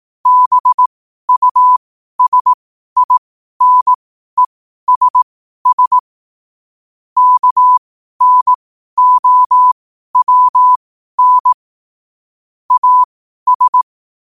Morse code, heard as a keyed tone, about 18 wpm, spaced as for 12 wpm, 1000 Hz. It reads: BUSINESS KNOWN AS
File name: qotdfiles / 12wpm